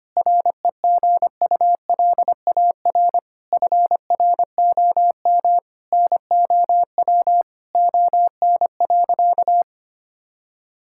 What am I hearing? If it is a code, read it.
Morse code, heard as a keyed tone, about 25 wpm, 700 Hz. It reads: REGULAR FROM NOW ON.
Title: Sentences from Top 1000 Words - Review Part 6 - 25wpm